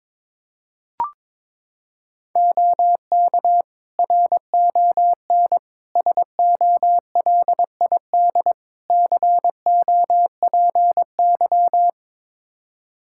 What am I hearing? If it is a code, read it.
Morse code, heard as a keyed tone, about 21 wpm, 700 Hz. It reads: OK RON SOLID COPY
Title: Practice QSO - Exchange 3 - More Info - 22wpm